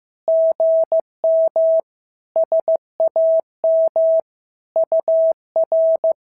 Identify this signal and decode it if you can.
Morse code, heard as a keyed tone, about 15 wpm, 650 Hz. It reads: GM SAM UR
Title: SOTA QSO Practice 15wpm